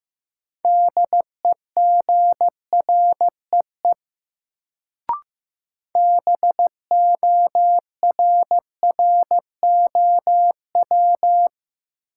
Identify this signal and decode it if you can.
Morse code, heard as a keyed tone, about 15 wpm, 700 Hz. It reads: DEGREE BORROW
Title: Sentences from Top 1500 Words - New Word Drill Alt - 15wpm